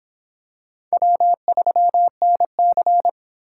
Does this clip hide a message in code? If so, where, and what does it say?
Yes: Morse code from 0.9 to 3.1 s, heard as a keyed tone, about 25 wpm, 700 Hz. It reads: W3NC